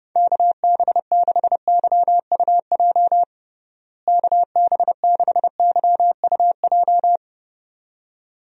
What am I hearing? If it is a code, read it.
Morse code, heard as a keyed tone, about 30 wpm, 700 Hz. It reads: KB6YUJ KB6YUJ
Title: USA Call Signs - 6 Characters - Repeated 2x - 30wpm